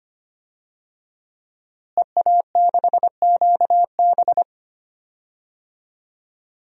Morse code, heard as a keyed tone, about 25 wpm, 700 Hz. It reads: EA6QB
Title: DXCC-100 Call Signs - 5 Characters - Repeated 2x - 25wpm